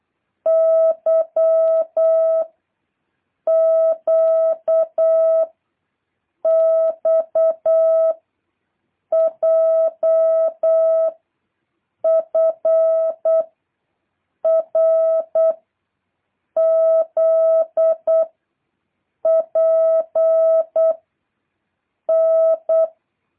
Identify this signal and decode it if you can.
Morse code, heard as a keyed tone, about 8 wpm, 650 Hz. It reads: Y Q X J F R Z P N
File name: jh8cbh / 2023cw